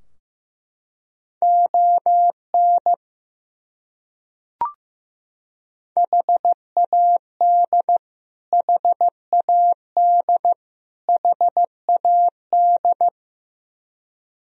Morse code, heard as a keyed tone, about 15 wpm, 700 Hz. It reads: ON HAD HAD HAD
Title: Top 100 Words Repeated 15wpm